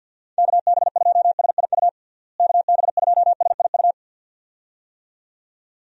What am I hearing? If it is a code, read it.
Morse code, heard as a keyed tone, about 51 wpm, 700 Hz. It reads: KB2SIU KB2SIU
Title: USA Call Signs - 6 Characters - Repeated 2x - 50wpm